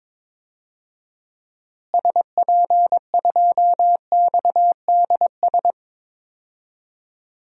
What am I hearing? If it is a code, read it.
Morse code, heard as a keyed tone, about 22 wpm, 700 Hz. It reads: SP2XDS
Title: DXCC-100 Call Signs - 6 Characters - Repeated 2x - 22wpm